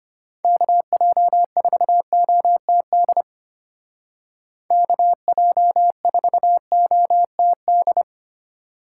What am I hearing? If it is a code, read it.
Morse code, heard as a keyed tone, about 30 wpm, 700 Hz. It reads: KJ4OTD KJ4OTD
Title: Speed Racing - USA Call Signs 25wpm